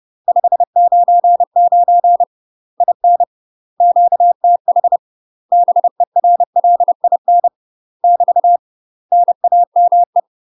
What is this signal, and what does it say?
Morse code, heard as a keyed tone, about 30 wpm, 700 Hz. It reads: 599 IN QTH BERLIN <BT> NAME
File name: UK MASTER QSOs TRAINING / 30WPM / UK MASTER QSOs 0019